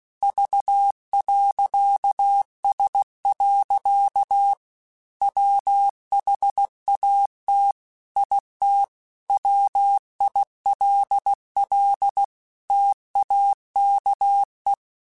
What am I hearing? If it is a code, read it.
Morse code, heard as a keyed tone, about 16 wpm, 800 Hz. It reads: V.S. WHAT IT WILL TAKE
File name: Technical Subjects / Technical Subjects-mp3s / OR Gate @15